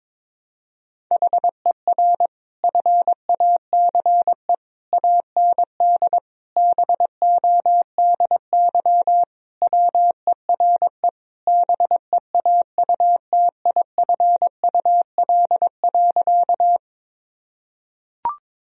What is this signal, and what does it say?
Morse code, heard as a keyed tone, about 22 wpm, 700 Hz. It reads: HER FACE AND BODY WERE BEAUTIFUL.
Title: Sentences from Top 1000 Words - Review Part 2 - 22wpm